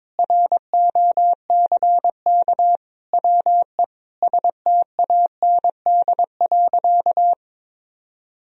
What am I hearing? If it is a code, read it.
Morse code, heard as a keyed tone, about 22 wpm, 700 Hz. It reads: ROCK WE STAND.
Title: Sentences from Top 1000 Words - Review Part 2 - 22wpm